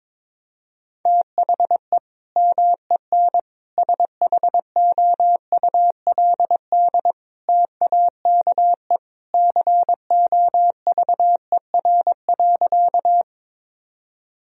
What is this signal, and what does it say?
Morse code, heard as a keyed tone, about 22 wpm, 700 Hz. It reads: THE MEN SHOULD TAKE COVER.
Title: Sentences from Top 1000 Words - Review Part 1 - 22wpm